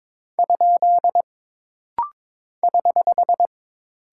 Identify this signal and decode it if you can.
Morse code, heard as a keyed tone, about 22 wpm, 700 Hz. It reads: ? <HH>
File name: practice / 22wpm / Single Punctuation-Prosign - Rapid-Fire - 22wpm